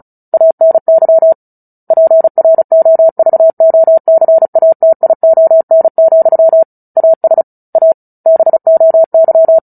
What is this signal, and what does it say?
Morse code, heard as a keyed tone, about 34 wpm, 650 Hz. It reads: ANY PROVOCATION, AS A BOY
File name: Book Two / 09-Chapter 9 - 35 WPM